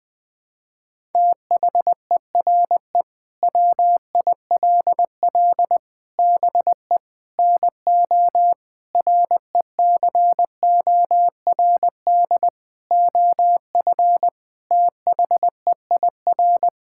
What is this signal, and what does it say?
Morse code, heard as a keyed tone, about 20 wpm, 700 Hz. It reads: THERE WILL BE NO RECORD OF THEIR
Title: Sentences from Top 1000 Words - Review Part 5 - 20wpm